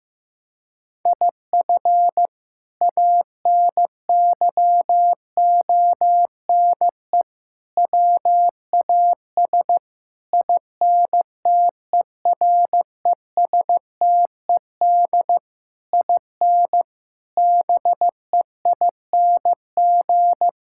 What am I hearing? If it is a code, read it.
Morse code, heard as a keyed tone, about 16 wpm, 700 Hz. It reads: IF ANYONE WAS INTERESTED IN BEING
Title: Sentences from Top 1600 Words 15wpm